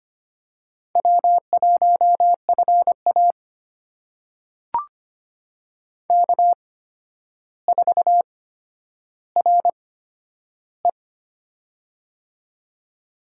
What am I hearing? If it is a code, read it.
Morse code, heard as a keyed tone, about 25 wpm, spaced as for 11 wpm, 700 Hz. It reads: W1FA K4RE
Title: USA Call Signs - 4 Characters - 8x Character Spacing - 25wpm